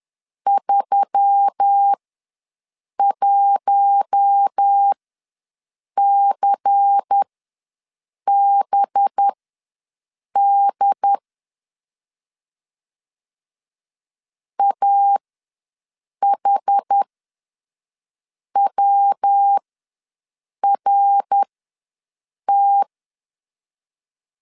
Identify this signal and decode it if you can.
Morse code, heard as a keyed tone, about 10 wpm, spaced as for 6 wpm, 800 Hz. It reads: 31CBD AHWRT
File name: LEZIONI / 5 LBVC / 5 LBVC ws 10wpm 3733